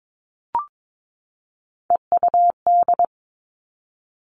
Morse code, heard as a keyed tone, about 22 wpm, 700 Hz. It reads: EUD
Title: Three Challenging Letter-Number Combinations 22wpm